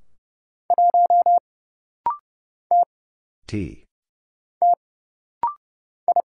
Morse code, heard as a keyed tone, about 31 wpm, 700 Hz. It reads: 1 T T I
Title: Single Letter-Number - Rapid-Fire 30wpm